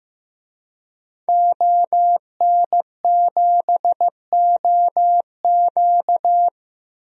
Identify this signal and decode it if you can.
Morse code, heard as a keyed tone, about 15 wpm, 700 Hz. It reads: ON7OQ
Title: DXCC-100 Call Signs - 5 Characters - 15wpm